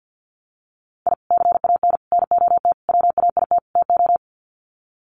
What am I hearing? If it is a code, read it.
Morse code, heard as a keyed tone, about 51 wpm, 700 Hz. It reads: I CAN NOT WAIT TO
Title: Sets of 5 Words Encore 50wpm